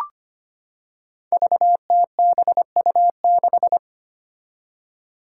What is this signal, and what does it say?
Morse code, heard as a keyed tone, about 25 wpm, 700 Hz. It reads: VTBU6
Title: Five Challenging Letter-Number Combinations 25wpm